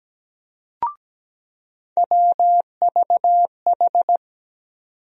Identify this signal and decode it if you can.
Morse code, heard as a keyed tone, about 17 wpm, 700 Hz. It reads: WVH
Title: USA Call Sign Postfixes - 3 Letters 17wpm